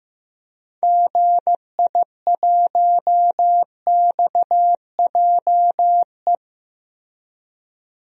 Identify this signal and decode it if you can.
Morse code, heard as a keyed tone, about 15 wpm, 700 Hz. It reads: GI1XJE
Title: DXCC-100 Call Signs - 6 Characters - Repeated 2x - 15wpm